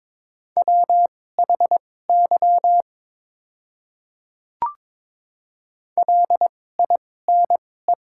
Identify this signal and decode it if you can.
Morse code, heard as a keyed tone, about 22 wpm, spaced as for 16 wpm, 700 Hz. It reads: WHY LINE
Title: 2x Character Spacing - Top 300 Words - 22wpm